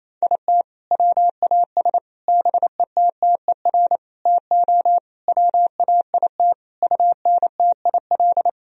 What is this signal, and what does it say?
Morse code, heard as a keyed tone, about 28 wpm, 700 Hz. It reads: IT WAS BETTER TO WAIT UNTIL
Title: Sentences from Top 1000 Words - Review Part 1 - 28wpm